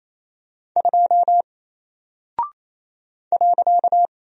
Morse code, heard as a keyed tone, about 28 wpm, 700 Hz. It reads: 2 .
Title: Single Character (Letter:Number:Punctuation:Prosign) 28wpm